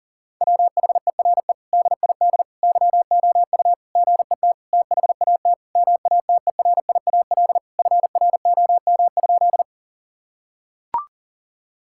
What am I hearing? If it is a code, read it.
Morse code, heard as a keyed tone, about 37 wpm, 700 Hz. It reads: WHERE DID YOU GET THAT MATERIAL FROM?
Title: Sentences from Top 1000 Words - Review Part 5 - 40wpm